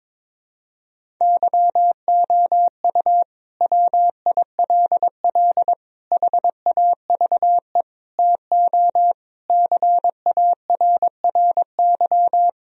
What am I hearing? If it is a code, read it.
Morse code, heard as a keyed tone, about 22 wpm, 700 Hz. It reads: YOU WILL HAVE TO CARRY
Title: Sentences from Top 1000 Words - Review Part 1 - 22wpm